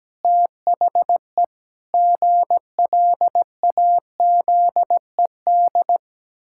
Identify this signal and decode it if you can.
Morse code, heard as a keyed tone, about 17 wpm, 700 Hz. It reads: THE GLAZED
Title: Pangram Challenge v2 - 17wpm